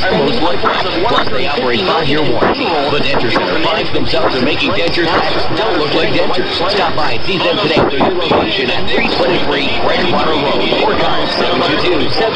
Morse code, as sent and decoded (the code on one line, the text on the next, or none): none